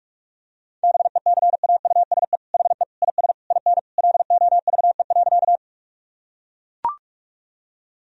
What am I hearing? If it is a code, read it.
Morse code, heard as a keyed tone, about 41 wpm, 700 Hz. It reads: BECAUSE HE IS IN LOVE.
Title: Sentences from Top 1000 Words - Review Part 3 - 45wpm